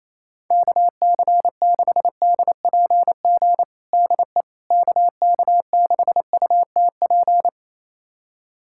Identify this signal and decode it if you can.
Morse code, heard as a keyed tone, about 28 wpm, 700 Hz. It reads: KC6DPG DE KK6UTP
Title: USA Callsign DE Callsign - 6 Characters - 28wpm